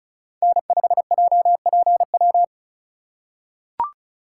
Morse code, heard as a keyed tone, about 36 wpm, 700 Hz. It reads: N5JPW
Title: USA Call Signs - 3x Character Spacing - 35wpm